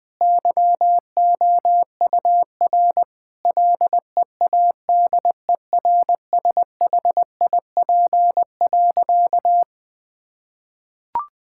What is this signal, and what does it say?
Morse code, heard as a keyed tone, about 20 wpm, 700 Hz. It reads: YOUR LEADERSHIP.
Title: Sentences from Top 1600 Words 20wpm